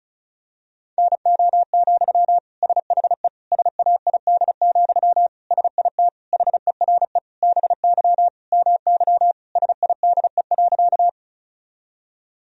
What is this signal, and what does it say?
Morse code, heard as a keyed tone, about 34 wpm, 700 Hz. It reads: NO, SHE SAID, SIT HERE BY MY SIDE.
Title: Sentences from Top 1000 Words - Review Part 1 - 35wpm